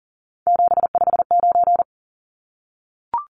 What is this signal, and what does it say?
Morse code, heard as a keyed tone, about 39 wpm, 700 Hz. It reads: 759
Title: Numbers - 3-Digit 40wpm